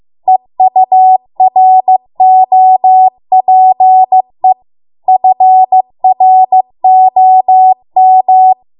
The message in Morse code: EUROPE FROM

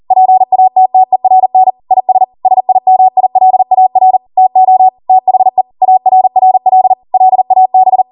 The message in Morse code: PATTERN IS SIMILAR TO THE ARRL LAB

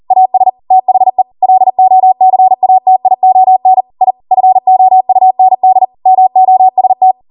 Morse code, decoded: AS THE LOCATION I FOUND MOST

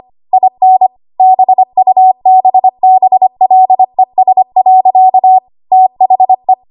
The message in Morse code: IN BUBBLES. THE

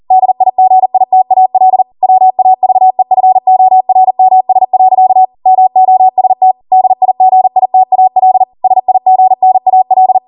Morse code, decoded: DIGITAL WAVEFORMS. MOST DIGITAL SIGNAL

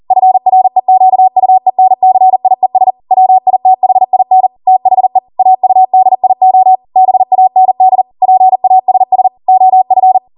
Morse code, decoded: FREQUENCIES WITHIN THE AUDIO BAND PASS OF

